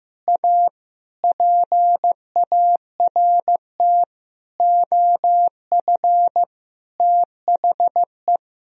Morse code, heard as a keyed tone, about 15 wpm, 700 Hz. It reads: A PART OF THE
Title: Sentences from Top 1000 Words - Review Part 4 - 15wpm